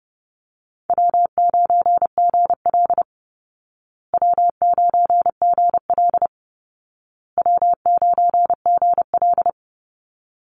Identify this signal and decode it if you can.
Morse code, heard as a keyed tone, about 30 wpm, 700 Hz. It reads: W9GL W9GL W9GL
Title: Easy Call Signs Repeated 3x - 30wpm